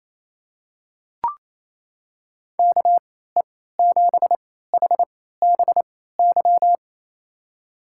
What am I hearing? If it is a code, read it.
Morse code, heard as a keyed tone, about 28 wpm, spaced as for 16 wpm, 700 Hz. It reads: KE7HBY
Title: USA Call Signs - 6 Characters - 3x Character Spacing - 28wpm